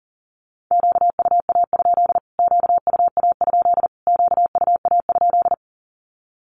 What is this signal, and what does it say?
Morse code, heard as a keyed tone, about 40 wpm, 700 Hz. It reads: QUA? QUA? QUA?
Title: Q-Codes Repeated 3x 40wpm